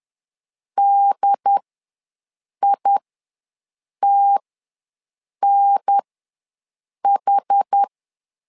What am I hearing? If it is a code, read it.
Morse code, heard as a keyed tone, about 10 wpm, spaced as for 6 wpm, 800 Hz. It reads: DITNH
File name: LEZIONI / 3 MWRO / 3 M ws 10wpm 2121